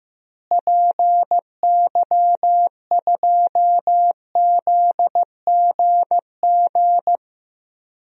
Morse code, heard as a keyed tone, about 15 wpm, 700 Hz. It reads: PY2ZGG
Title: DXCC-100 Call Signs - 6 Characters - Repeated 2x - 15wpm